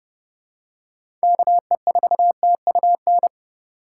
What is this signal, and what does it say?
Morse code, heard as a keyed tone, about 30 wpm, 700 Hz. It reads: KE4TUN
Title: USA Call Signs - 6 Characters - 2x Character Spacing - 30wpm